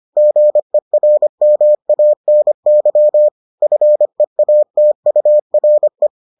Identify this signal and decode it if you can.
Morse code, heard as a keyed tone, about 25 wpm, 600 Hz. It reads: GERMANY FEATURE